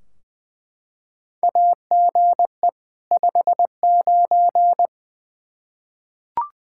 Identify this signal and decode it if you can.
Morse code, heard as a keyed tone, about 20 wpm, 700 Hz. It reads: AGE 59
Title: K1USN SST QSO Practice 20wpm